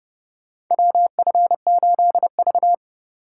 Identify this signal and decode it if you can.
Morse code, heard as a keyed tone, about 30 wpm, 700 Hz. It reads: WF8V
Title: USA Call Signs - 4 Characters - 3x Character Spacing - 30wpm